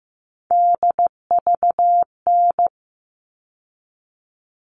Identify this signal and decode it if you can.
Morse code, heard as a keyed tone, about 15 wpm, 700 Hz. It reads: DVN
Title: All Cut Numbers - 3-Digit 15wpm